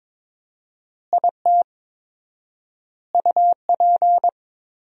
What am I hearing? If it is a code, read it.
Morse code, heard as a keyed tone, about 21 wpm, 700 Hz. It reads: IT UP
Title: 4x Word-Spacing - Sentences from Top 400 Words - 22wpm